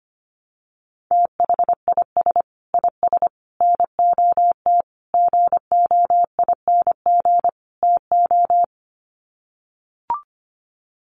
Morse code, heard as a keyed tone, about 25 wpm, 700 Hz. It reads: THIS IS NOT GOING TO
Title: Sets of 5 Words Encore2 25wpm